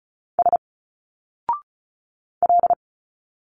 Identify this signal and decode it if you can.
Morse code, heard as a keyed tone, about 34 wpm, 700 Hz. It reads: S L
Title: Single Letters 35wpm